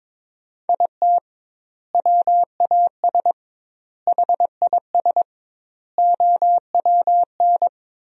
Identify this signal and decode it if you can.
Morse code, heard as a keyed tone, about 22 wpm, 700 Hz. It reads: IT WAS HIS OWN
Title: 2x Word-Spacing - Sentences from Top 300 Words - 22wpm